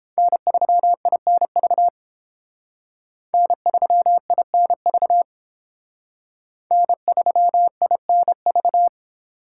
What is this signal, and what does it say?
Morse code, heard as a keyed tone, about 28 wpm, 700 Hz. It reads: N3INV N3INV N3INV